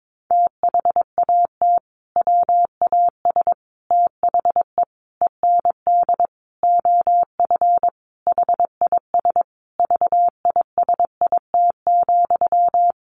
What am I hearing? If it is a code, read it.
Morse code, heard as a keyed tone, about 22 wpm, 700 Hz. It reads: THAT WAS THE END OF HIS VISIT,
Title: Sentences from Top 1400 Words 22wpm